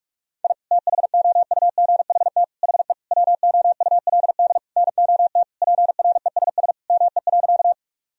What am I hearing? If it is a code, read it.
Morse code, heard as a keyed tone, about 42 wpm, 700 Hz. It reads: I THOUGHT HE WOULD NOT PRESS ME.
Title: Sentences from Top 1000 Words - Review Part 1 - 45wpm